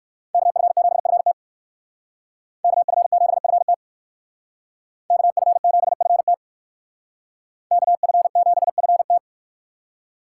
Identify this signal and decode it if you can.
Morse code, heard as a keyed tone, about 58 wpm, 700 Hz. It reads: KF7FT KF7FT KF7FT KF7FT